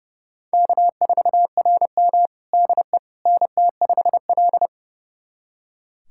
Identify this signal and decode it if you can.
Morse code, heard as a keyed tone, about 30 wpm, 700 Hz. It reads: K4RM DE NT5L
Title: USA Callsign DE Callsign - 4 Characters - 30wpm